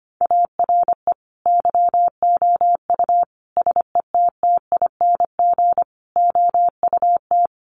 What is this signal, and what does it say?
Morse code, heard as a keyed tone, about 25 wpm, 700 Hz. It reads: ARE YOU SETTING OUT
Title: Sentences from Top 1300 Words 25wpm